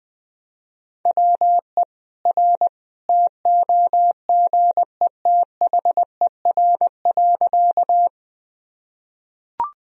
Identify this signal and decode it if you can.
Morse code, heard as a keyed tone, about 21 wpm, 700 Hz. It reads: WE R TOGETHER.
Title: Sentences from Top 300 Words and Abbreviations 20wpm